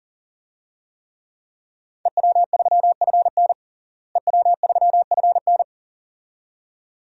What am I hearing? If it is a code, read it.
Morse code, heard as a keyed tone, about 40 wpm, 700 Hz. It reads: EW3FN EW3FN
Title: DXCC-100 Call Signs - 5 Characters - Repeated 2x - 40wpm